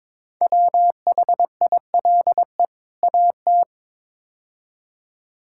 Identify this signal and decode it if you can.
Morse code, heard as a keyed tone, about 22 wpm, 700 Hz. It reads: WHILE AT
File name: practice / 22wpm / Build A Phrase - 5-Word Phrases - 22wpm